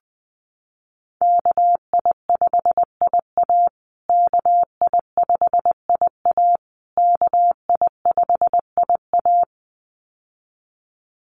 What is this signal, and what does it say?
Morse code, heard as a keyed tone, about 19 wpm, 700 Hz. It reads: KI5IA KI5IA KI5IA
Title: USA Call Signs Repeated 3x 20wpm